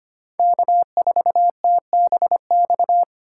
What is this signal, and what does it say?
Morse code, heard as a keyed tone, about 25 wpm, 700 Hz. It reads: K4TBX
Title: USA Call Signs - 5 Characters - 3x Character Spacing - 25wpm